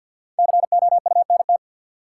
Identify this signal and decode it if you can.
Morse code, heard as a keyed tone, about 51 wpm, 700 Hz. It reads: COUNT
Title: Sentences from Top 1500 Words - New Word Drill Alt - 50wpm